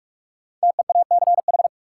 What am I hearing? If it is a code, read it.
Morse code, heard as a keyed tone, about 42 wpm, 700 Hz. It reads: TEACH
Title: Sentences from Top 500 Words - New Word Drill - 45wpm